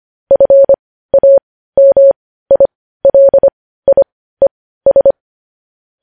Morse code, heard as a keyed tone, about 25 wpm, spaced as for 15 wpm, 550 Hz. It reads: FAMILIES
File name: podcast / News headlines-1x-25x15